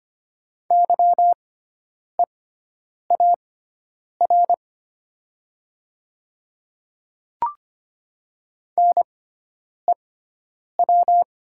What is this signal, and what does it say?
Morse code, heard as a keyed tone, about 25 wpm, spaced as for 9 wpm, 700 Hz. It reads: YEAR NEW
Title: 6x Character Spacing - Top 200 Words - 25wpm